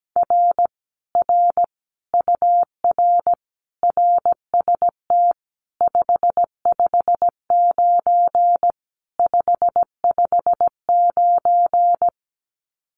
Morse code, heard as a keyed tone, about 17 wpm, 700 Hz. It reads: R R UR RST 559 559
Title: Common QSO Phrases Encore 17wpm